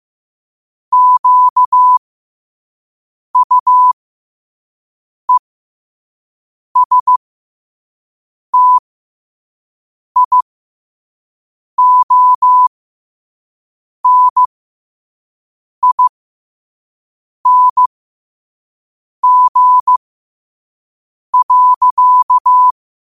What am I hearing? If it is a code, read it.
Morse code, heard as a keyed tone, about 15 wpm, spaced as for 5 wpm, 1000 Hz. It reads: QUESTIONING.
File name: qotdfiles / 5wpm